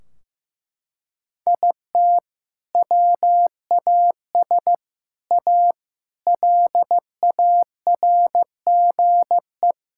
Morse code, heard as a keyed tone, about 15 wpm, 700 Hz. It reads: IT WAS A LARGE
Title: Sentences from Top 1600 Words 15wpm